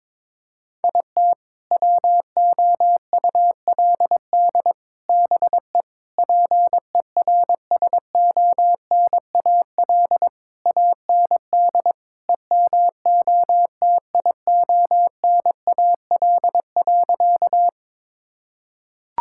Morse code, heard as a keyed tone, about 22 wpm, 700 Hz. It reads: IT WOULD BE PERSONAL AND EMOTIONAL.
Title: Sentences from Top 1000 Words - Review Part 5 - 22wpm